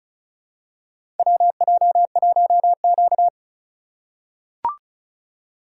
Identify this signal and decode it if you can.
Morse code, heard as a keyed tone, about 36 wpm, 700 Hz. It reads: WJ1Q
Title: USA Call Signs - 4 Characters - 4x Character Spacing - 35wpm